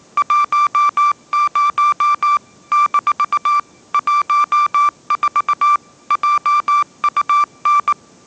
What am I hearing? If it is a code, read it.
Morse code, heard as a keyed tone, about 19 wpm, 1180 Hz. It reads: 10 - 14 JUN